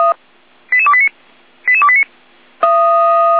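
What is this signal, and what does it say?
Five-tone selective call (ZVEI), heard as quick runs of five tones, tones of about 70 ms. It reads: 88289 88289
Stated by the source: Five-tone ZVEI selective call received on SDR.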